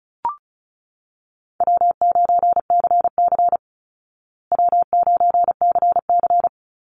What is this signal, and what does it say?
Morse code, heard as a keyed tone, about 34 wpm, 700 Hz. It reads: W9CC W9CC
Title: Easy Call Signs Repeated 3x - 35wpm